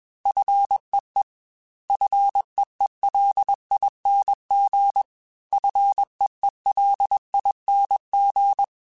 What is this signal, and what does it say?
Morse code, heard as a keyed tone, about 21 wpm, 800 Hz. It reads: FEE FEELING FEELING
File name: Fs / Single F mp3s / Single F Prefixes 3x2 Twice @20